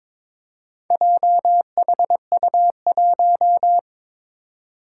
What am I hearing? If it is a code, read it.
Morse code, heard as a keyed tone, about 22 wpm, 700 Hz. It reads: JHU1
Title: Four Challenging Letter-Number Combinations 22wpm